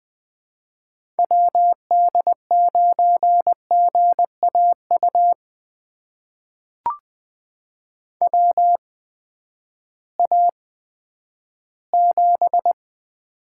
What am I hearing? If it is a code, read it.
Morse code, heard as a keyed tone, about 21 wpm, spaced as for 11 wpm, 700 Hz. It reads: WD9GAU WA7
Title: USA Call Signs - 6 Characters - 8x Character Spacing - 20wpm